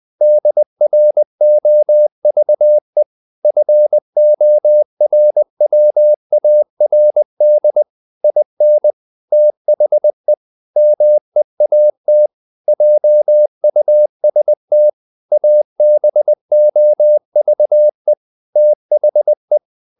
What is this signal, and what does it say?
Morse code, heard as a keyed tone, about 20 wpm, 600 Hz. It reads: DROVE FORWARD IN THE MEAT JUST ABOVE THE
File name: the old man and the sea part 2 20wpm